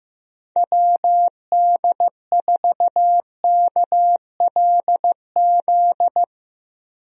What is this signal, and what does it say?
Morse code, heard as a keyed tone, about 15 wpm, 700 Hz. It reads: WD4KLZ
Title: USA Call Signs - 6 Characters - Repeated 3x - 15wpm